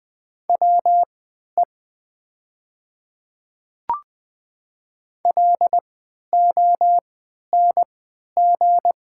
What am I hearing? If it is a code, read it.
Morse code, heard as a keyed tone, about 20 wpm, spaced as for 11 wpm, 700 Hz. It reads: WE LONG
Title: 3x Character Spacing - Top 100 Words - 20wpm